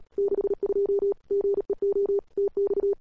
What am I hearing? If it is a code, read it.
Morse code, heard as a keyed tone, about 38 wpm, 400 Hz. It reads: 62 GEO TX
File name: cw-sprint-k5tr-op-n5rz